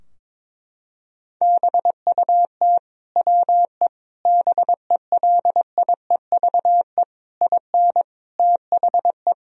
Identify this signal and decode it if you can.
Morse code, heard as a keyed tone, about 21 wpm, 700 Hz. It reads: BUT WE BELIEVE IN THE
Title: Sentences from Top 1000 Words - Review Part 3 - 22wpm